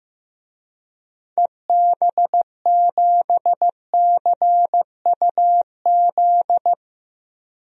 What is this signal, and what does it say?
Morse code, heard as a keyed tone, about 15 wpm, 700 Hz. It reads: EB7CUZ
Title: DXCC-100 Call Signs - 6 Characters - Repeated 2x - 15wpm